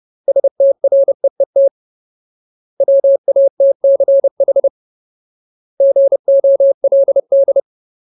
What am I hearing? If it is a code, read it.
Morse code, heard as a keyed tone, about 30 wpm, 550 Hz. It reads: STREET WATCH GOLD
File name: Ultimate Word List / 30WPM Ultimate